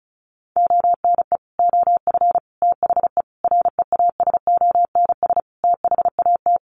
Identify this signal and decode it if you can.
Morse code, heard as a keyed tone, about 34 wpm, 700 Hz. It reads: ONE OF THE REASONS THAT
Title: Sets of 5 Words Encore 35wpm